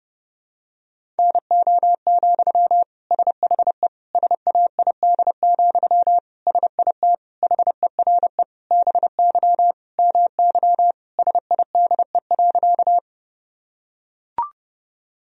Morse code, heard as a keyed tone, about 32 wpm, 700 Hz. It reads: NO, SHE SAID, SIT HERE BY MY SIDE.
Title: Sentences from Top 1000 Words - Review Part 1 - 30wpm